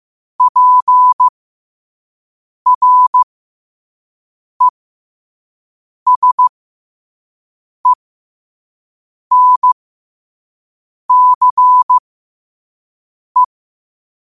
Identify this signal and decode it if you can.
Morse code, heard as a keyed tone, about 15 wpm, spaced as for 5 wpm, 1000 Hz. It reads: PRESENCE
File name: qotdfiles / 5wpm